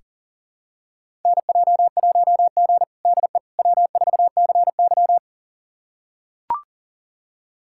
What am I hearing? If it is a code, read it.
Morse code, heard as a keyed tone, about 37 wpm, 700 Hz. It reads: NJ1G DE W4CY
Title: USA Callsign DE Callsign - 4 Characters - 40wpm